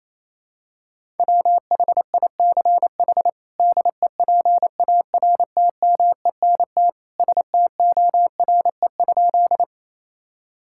Morse code, heard as a keyed tone, about 28 wpm, 700 Hz. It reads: WHICH DEPARTMENT STORE?
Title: Sentences from Top 1000 Words - Review Part 5 - 28wpm